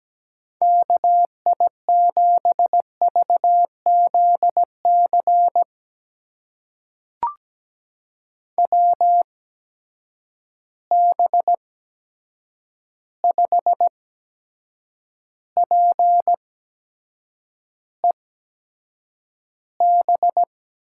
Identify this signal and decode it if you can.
Morse code, heard as a keyed tone, about 17 wpm, spaced as for 7 wpm, 700 Hz. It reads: KI7VZC WB5PEB
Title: USA Call Signs - 6 Characters - 8x Character Spacing - 17wpm